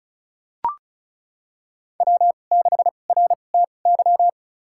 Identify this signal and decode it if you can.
Morse code, heard as a keyed tone, about 36 wpm, spaced as for 25 wpm, 700 Hz. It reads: W6RTY
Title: USA Call Signs - 2x Character Spacing - 35wpm